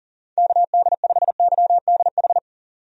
Morse code, heard as a keyed tone, about 40 wpm, 700 Hz. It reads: KD5YDH
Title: USA Call Signs - 6 Characters - 40wpm